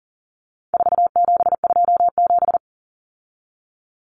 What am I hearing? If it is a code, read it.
Morse code, heard as a keyed tone, about 40 wpm, 700 Hz. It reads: 4727
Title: Numbers - 4-Digit 40wpm